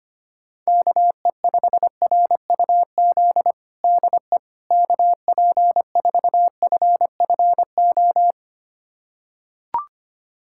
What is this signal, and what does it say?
Morse code, heard as a keyed tone, about 25 wpm, 700 Hz. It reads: KE5RUZ DE KP4FFO
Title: USA Callsign DE Callsign - 6 Characters - 25wpm